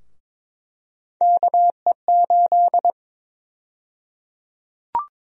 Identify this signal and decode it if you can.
Morse code, heard as a keyed tone, about 23 wpm, 700 Hz. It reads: KE8
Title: USA Call Sign Prefixes - 22wpm